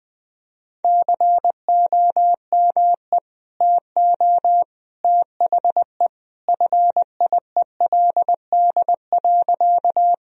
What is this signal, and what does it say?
Morse code, heard as a keyed tone, about 21 wpm, 700 Hz. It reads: COME TO THE FIELD.
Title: Sentences from Top 1000 Words - Review Part 2 - 20wpm